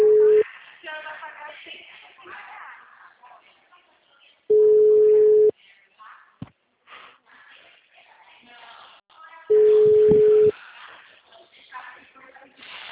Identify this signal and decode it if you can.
Ringback tone (Europe, 425 Hz), with speech.
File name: out- -2050- - - .54401